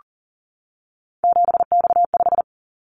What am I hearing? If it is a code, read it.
Morse code, heard as a keyed tone, about 42 wpm, 700 Hz. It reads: 7X5
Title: Callsign Prefixes for Top 100 DXCC - 40wpm